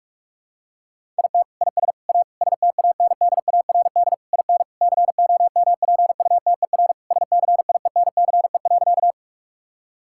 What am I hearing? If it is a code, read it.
Morse code, heard as a keyed tone, about 46 wpm, 700 Hz. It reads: IT IS A STANDARD IN COMPUTER SCIENCE.
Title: Sentences from Top 1000 Words - Review Part 6 - 45wpm